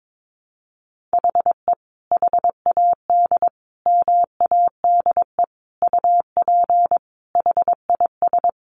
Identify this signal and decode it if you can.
Morse code, heard as a keyed tone, about 22 wpm, 700 Hz. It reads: HE HAD MADE UP HIS
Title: Sentences from Top 500 Words 22wpm